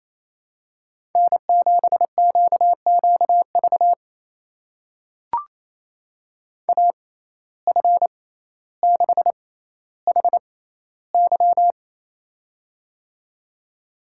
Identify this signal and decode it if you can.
Morse code, heard as a keyed tone, about 28 wpm, spaced as for 14 wpm, 700 Hz. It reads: N7QQV AF6HY
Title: USA Call Signs - 5 Characters - 6x Character Spacing - 28wpm